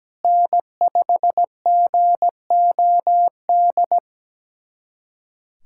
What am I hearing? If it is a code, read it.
Morse code, heard as a keyed tone, about 17 wpm, 700 Hz. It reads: N5GOD
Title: USA Callsign DE Callsign - 5 Characters - 17wpm